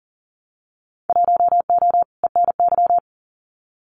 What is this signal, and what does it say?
Morse code, heard as a keyed tone, about 40 wpm, 700 Hz. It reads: 1O ENY
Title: Winter Field Day Exchanges 40wpm